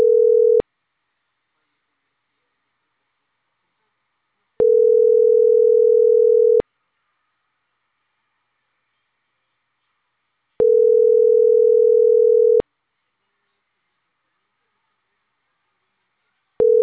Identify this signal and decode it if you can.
Ringback tone (North America), 440 + 480 Hz.